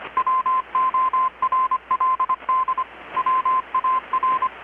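Morse code, heard as a keyed tone, about 25 wpm, 1040 Hz, with radio noise. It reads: WORLD WAR
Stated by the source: Morse from KPH's Night of Nights XX maritime radiotelegraphy event, received in Chiapas, Mexico about 3,800 km away on a Kenwood R-600 with a grounded 10 m long-wire antenna.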